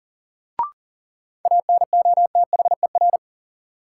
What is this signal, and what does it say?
Morse code, heard as a keyed tone, about 40 wpm, 700 Hz. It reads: ANOTHER
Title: Sets of 5 Words Encore - New Word Drill - 40wpm